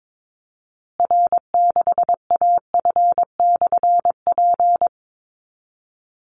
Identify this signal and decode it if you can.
Morse code, heard as a keyed tone, about 22 wpm, 700 Hz. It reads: R6AF/P
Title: Call Signs Repeated 2x - 22wpm